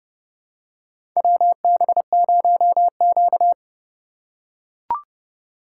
Morse code, heard as a keyed tone, about 29 wpm, 700 Hz. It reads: WB0Q
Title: USA Call Signs - 4x Character Spacing - 30wpm